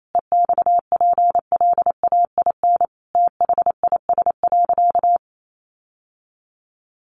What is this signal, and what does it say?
Morse code, heard as a keyed tone, about 28 wpm, 700 Hz. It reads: EXPLAIN THIS.
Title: Sentences from Top 1100 Words 28wpm